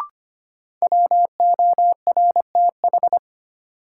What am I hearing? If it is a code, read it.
Morse code, heard as a keyed tone, about 25 wpm, 700 Hz. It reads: WORTH
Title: Sentences from Top 800 Words - New Word Drill - 25wpm